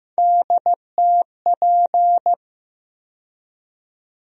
Morse code, heard as a keyed tone, about 15 wpm, 700 Hz. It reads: DTP
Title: USA Call Sign Postfixes - 3 Letters 15wpm